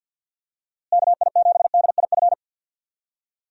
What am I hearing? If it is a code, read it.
Morse code, heard as a keyed tone, about 47 wpm, 700 Hz. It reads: KI7DIF
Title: USA Call Signs - 6 Characters - Repeated 3x - 50wpm